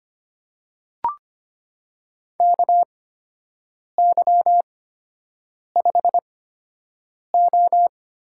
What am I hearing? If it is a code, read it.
Morse code, heard as a keyed tone, about 25 wpm, spaced as for 7 wpm, 700 Hz. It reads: KY5O
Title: USA Call Signs - 4 Characters - 8x Character Spacing - 25wpm